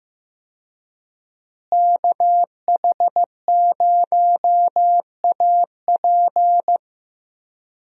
Morse code, heard as a keyed tone, about 16 wpm, 700 Hz. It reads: KH0AP
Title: USA Call Signs - 5 Characters - Repeated 3x - 15wpm